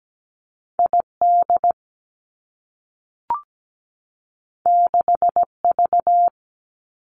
Morse code, heard as a keyed tone, about 17 wpm, 700 Hz. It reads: ID 6V
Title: Two Challenging Letter-Number Combinations 17wpm